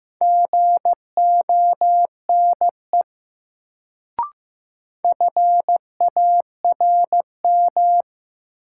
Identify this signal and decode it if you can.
Morse code, heard as a keyed tone, about 15 wpm, 700 Hz. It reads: GONE FARM
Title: Sentences from Top 1200 Words - New Word Drill Alt - 15wpm